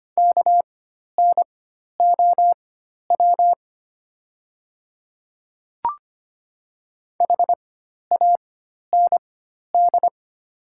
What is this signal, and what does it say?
Morse code, heard as a keyed tone, about 25 wpm, spaced as for 12 wpm, 700 Hz. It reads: KNOW HAND
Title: 4x Character Spacing - Top 300 Words - 25wpm